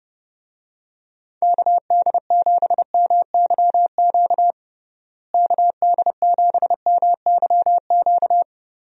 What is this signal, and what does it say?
Morse code, heard as a keyed tone, about 32 wpm, 700 Hz. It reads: KD7MYQ KD7MYQ
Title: USA Call Signs - 6 Characters - Repeated 3x - 30wpm